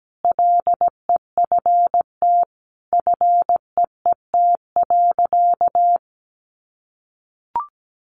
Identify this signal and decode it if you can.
Morse code, heard as a keyed tone, about 17 wpm, 700 Hz. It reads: LEFT FEET.
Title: Sentences from Top 300 Words 17wpm